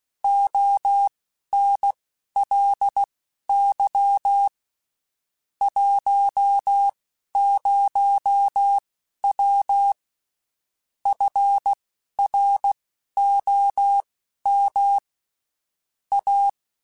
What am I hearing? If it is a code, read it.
Morse code, heard as a keyed tone, about 16 wpm, spaced as for 11 wpm, 800 Hz. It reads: ONLY 10W FROM A
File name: (spaced) / QSOs Spaced-mp3s / kc7haa @15 sp